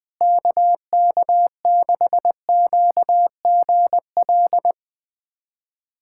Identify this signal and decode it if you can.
Morse code, heard as a keyed tone, about 20 wpm, 700 Hz. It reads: KK6QGL
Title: USA Callsign DE Callsign 20wpm